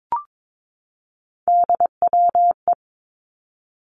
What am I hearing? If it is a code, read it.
Morse code, heard as a keyed tone, about 22 wpm, 700 Hz. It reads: DWE
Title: Three Challenging Letter-Number Combinations 22wpm